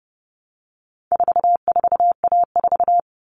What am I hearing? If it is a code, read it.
Morse code, heard as a keyed tone, about 30 wpm, 700 Hz. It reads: 44A4
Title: All Cut Numbers - 4-Digit 30wpm